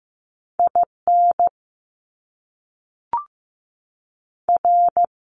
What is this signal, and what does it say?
Morse code, heard as a keyed tone, about 15 wpm, 700 Hz. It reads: I N R
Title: Top Two Letter Combinations in Words 15wpm